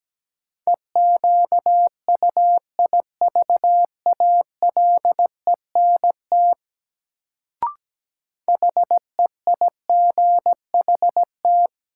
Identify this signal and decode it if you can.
Morse code, heard as a keyed tone, about 17 wpm, 700 Hz. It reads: EQUIVALENT HEIGHT
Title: Sentences from Top 1600 Words - New Word Drill - 17wpm